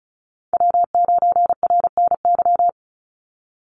Morse code, heard as a keyed tone, about 36 wpm, 700 Hz. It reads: W9RNY
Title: Call Signs - 35wpm